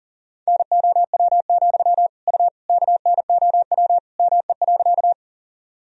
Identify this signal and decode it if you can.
Morse code, heard as a keyed tone, about 40 wpm, 700 Hz. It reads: NOW, U KNOW ME.
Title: Sentences from Top 200 Words and Abbreviations 40wpm